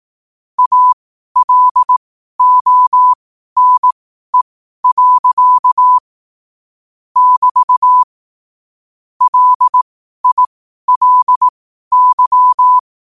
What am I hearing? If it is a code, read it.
Morse code, heard as a keyed tone, about 18 wpm, spaced as for 12 wpm, 1000 Hz. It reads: ALONE. <BT> LILY
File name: qotdfiles / 12wpm